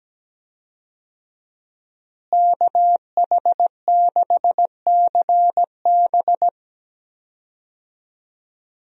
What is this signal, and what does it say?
Morse code, heard as a keyed tone, about 17 wpm, 700 Hz. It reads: KH6CB
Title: USA Call Signs - 5 Characters - Repeated 2x - 17wpm